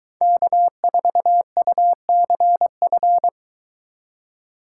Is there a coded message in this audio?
Yes: Morse code from 0.2 to 3.3 s, heard as a keyed tone, about 23 wpm, 700 Hz. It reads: K4UCF